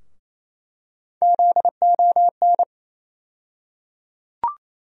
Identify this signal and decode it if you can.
Morse code, heard as a keyed tone, about 28 wpm, 700 Hz. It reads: ZON
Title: USA Call Sign Postfixes - 3 Letters 28wpm